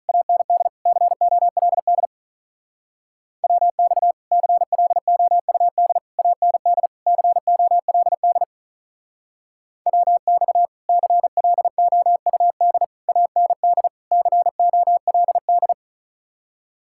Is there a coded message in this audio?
Yes: Morse code from 0.1 to 15.7 s, heard as a keyed tone, about 44 wpm, 700 Hz. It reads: AND COLD WX CLOUD AND COLD WX CLOUD AND COLD